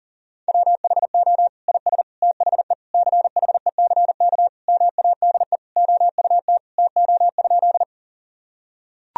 Morse code, heard as a keyed tone, about 40 wpm, 700 Hz. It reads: WHO IS THE CHECK MADE OUT TO?
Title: Sentences from Top 1000 Words - Review Part 3 - 40wpm